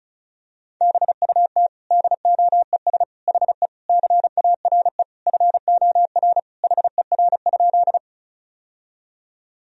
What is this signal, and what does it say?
Morse code, heard as a keyed tone, about 36 wpm, 700 Hz. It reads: BUT DOES HE CARE FOR HER?
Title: Sentences from Top 1000 Words - Review Part 1 - 35wpm